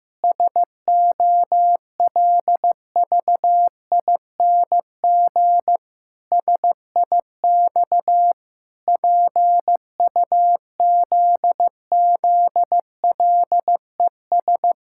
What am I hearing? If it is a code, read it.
Morse code, heard as a keyed tone, about 15 wpm, 700 Hz. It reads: SOLVING SIX PUZZLES
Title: Pangram Challenge v2 - 15wpm